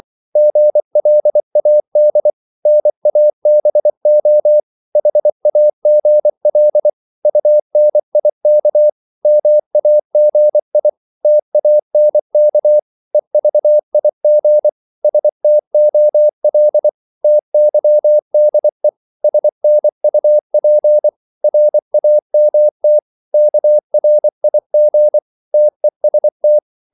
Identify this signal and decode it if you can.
Morse code, heard as a keyed tone, about 24 wpm, 600 Hz. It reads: GLAD NABO HAGL UNIK MAGI TANK EVIG STOL TYDE SNUP RAMT KRIG TEST
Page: Korte danske ord 24wpm